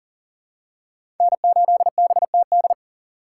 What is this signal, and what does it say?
Morse code, heard as a keyed tone, about 40 wpm, 700 Hz. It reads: N8BTD
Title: USA Call Signs - 5 Characters - 3x Character Spacing - 40wpm